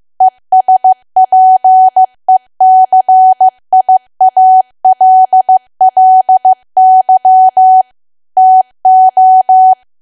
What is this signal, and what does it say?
Morse code, heard as a keyed tone, about 15 wpm, 750 Hz. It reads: ESPECIALLY TO
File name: cwa / W1AW-Level2 / 15WPM